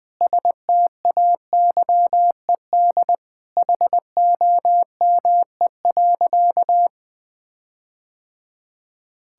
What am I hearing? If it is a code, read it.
Morse code, heard as a keyed tone, about 20 wpm, 700 Hz. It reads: STAYED HOME.
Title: Sentences with 4-Word Phrases - 20wpm